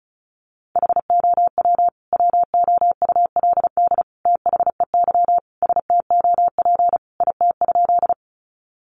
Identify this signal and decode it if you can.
Morse code, heard as a keyed tone, about 34 wpm, 700 Hz. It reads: HOW WOULD THEY STOP IT?
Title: Sentences from Top 300 Words 35wpm